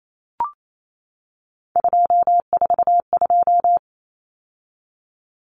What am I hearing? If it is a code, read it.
Morse code, heard as a keyed tone, about 28 wpm, 700 Hz. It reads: 242
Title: Numbers - 3-Digit 28wpm